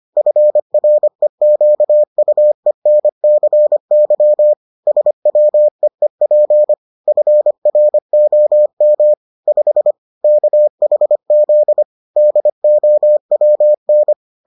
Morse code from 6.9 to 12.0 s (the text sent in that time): FROM 5 KHZ